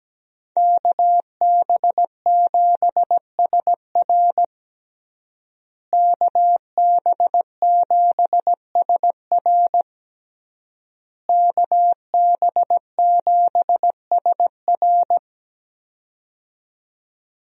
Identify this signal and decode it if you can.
Morse code, heard as a keyed tone, about 17 wpm, 700 Hz. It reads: KB7SR KB7SR KB7SR
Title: USA Call Signs - 5 Characters - Repeated 3x - 17wpm